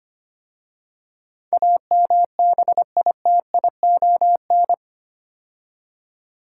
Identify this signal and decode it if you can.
Morse code, heard as a keyed tone, about 25 wpm, 700 Hz. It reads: AMBITION
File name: 8 Letter Words - First 3 Letters Given - 25wpm